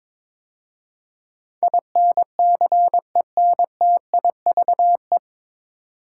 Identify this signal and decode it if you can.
Morse code, heard as a keyed tone, about 22 wpm, 700 Hz. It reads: INCENTIVE
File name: 9 Letter Words - First 3 Letters Given - 22wpm